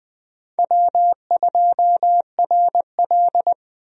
Morse code, heard as a keyed tone, about 21 wpm, 700 Hz. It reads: W2RL
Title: USA Call Signs - 4 Characters - 20wpm